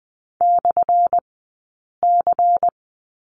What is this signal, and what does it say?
Morse code, heard as a keyed tone, about 20 wpm, spaced as for 8 wpm, 700 Hz. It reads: /C
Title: 177 Lesson 29d - Slash - Two Characters 20wpm